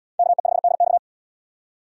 Morse code, heard as a keyed tone, about 77 wpm, 700 Hz. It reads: KE5DX